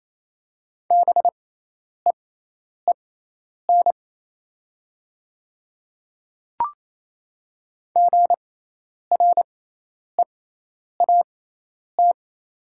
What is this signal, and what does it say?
Morse code, heard as a keyed tone, about 28 wpm, spaced as for 10 wpm, 690 Hz. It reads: BEEN GREAT
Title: 6x Character Spacing - Top 200 Words - 28wpm